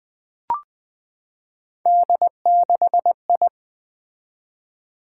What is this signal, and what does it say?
Morse code, heard as a keyed tone, about 20 wpm, 700 Hz. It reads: D6I
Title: Three Letter-Number Combinations - 20wpm